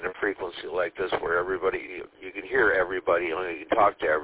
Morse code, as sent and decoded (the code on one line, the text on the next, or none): none